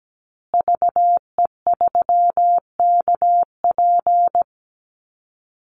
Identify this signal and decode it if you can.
Morse code, heard as a keyed tone, about 17 wpm, 700 Hz. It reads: VE3KP
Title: Call Signs Repeated 3x - 17wpm